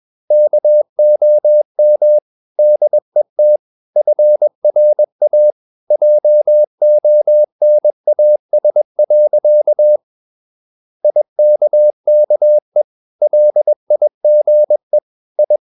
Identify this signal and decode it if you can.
Morse code, heard as a keyed tone, about 21 wpm, 600 Hz. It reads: KOM DET FRA JONAS. IKKE LIGE I
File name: Ferie QRS 21wpm